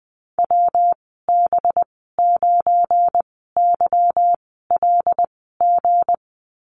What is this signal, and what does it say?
Morse code, heard as a keyed tone, about 20 wpm, spaced as for 15 wpm, 700 Hz. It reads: WB9YLG
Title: 193 Lesson 31g - Letter v - Call Signs 20wpm